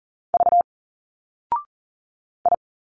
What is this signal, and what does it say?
Morse code, heard as a keyed tone, about 40 wpm, 700 Hz. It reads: V I
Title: Single Letters 40wpm